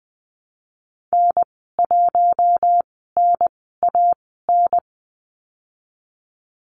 Morse code, heard as a keyed tone, about 20 wpm, spaced as for 14 wpm, 700 Hz. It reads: N1NAN
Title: 042 Lesson 8g - Number 1 - Call Signs 20wpm